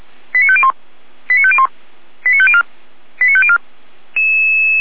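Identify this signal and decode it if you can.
Five-tone selective call (ZVEI), heard as quick runs of five tones, tones of about 80 ms. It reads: 89501 89501 89504 89504
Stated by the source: Five-tone ZVEI selective call 89004, received by SDR.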